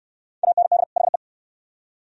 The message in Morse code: AND HE

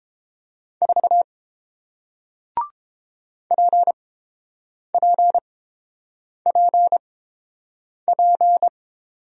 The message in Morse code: 4 PPPP

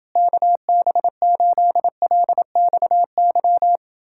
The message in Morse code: KB8LXY

